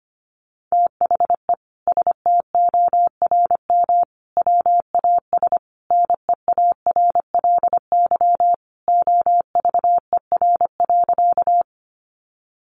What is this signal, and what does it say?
Morse code, heard as a keyed tone, about 25 wpm, 700 Hz. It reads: THE STORM WAS NEARLY OVER.
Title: Sentences from Top 1500 Words 25wpm